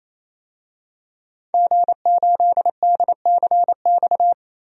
Morse code, heard as a keyed tone, about 28 wpm, 700 Hz. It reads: G8DCX
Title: DXCC-100 Call Signs - 5 Characters - 28wpm